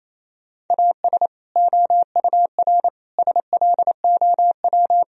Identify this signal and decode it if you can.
Morse code, heard as a keyed tone, about 27 wpm, 700 Hz. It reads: AS OUR SLOW
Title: Sentences from Top 1000 Words - Review Part 2 - 28wpm